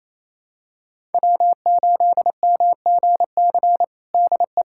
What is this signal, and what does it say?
Morse code, heard as a keyed tone, about 28 wpm, 700 Hz. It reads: W8MGC DE
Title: USA Callsign DE Callsign - 5 Characters - 28wpm